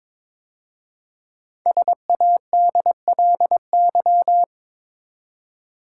Morse code, heard as a keyed tone, about 22 wpm, 700 Hz. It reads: SADLY
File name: 5 Letter Words - First 3 Letters Given - 22wpm